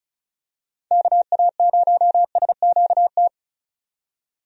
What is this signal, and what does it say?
Morse code, heard as a keyed tone, about 36 wpm, 700 Hz. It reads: KA0SQT
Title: USA Call Signs - 6 Characters - 35wpm